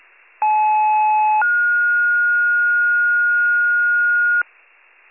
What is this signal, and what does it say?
Two-tone page, heard as a one-second tone, then a three-second tone, about 860 then 1430 Hz, with noise. It is heard from 0.4 to 4.4 s.